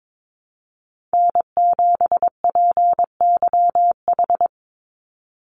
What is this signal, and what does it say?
Morse code, heard as a keyed tone, about 22 wpm, 700 Hz. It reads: N7PYH
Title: USA Call Signs 22wpm